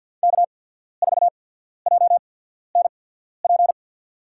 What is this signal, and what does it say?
Morse code, heard as a keyed tone, about 53 wpm, spaced as for 14 wpm, 700 Hz. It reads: K4JNP
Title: USA Call Signs - 5 Characters - 8x Character Spacing - 50wpm